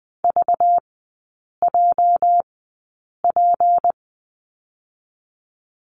Morse code, heard as a keyed tone, about 20 wpm, spaced as for 8 wpm, 700 Hz. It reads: VJP
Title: 218 Lesson 35e - Number 0 - Three Characters 20wpm